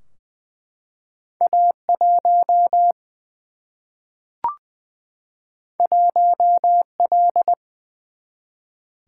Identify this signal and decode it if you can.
Morse code, heard as a keyed tone, about 20 wpm, 700 Hz. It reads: A 1 1 L
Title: Single-Letter Single-Number 20wpm